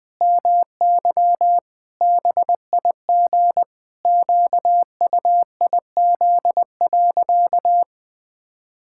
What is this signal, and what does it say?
Morse code, heard as a keyed tone, about 20 wpm, 700 Hz. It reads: MY BIG QUIZ.
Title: Pangram Challenge v2 - 20wpm